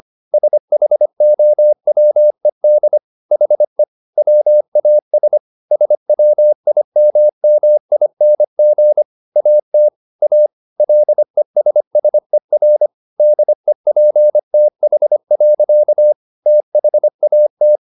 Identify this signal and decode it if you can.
Morse code, heard as a keyed tone, about 25 wpm, 600 Hz. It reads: SHOWED HE WAS SWIMMING AT A LESSER DEPTH. THAT
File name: the old man and the sea part 1 25wpm